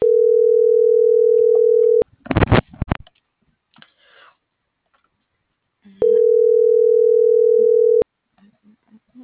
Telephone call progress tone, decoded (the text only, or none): ringback tone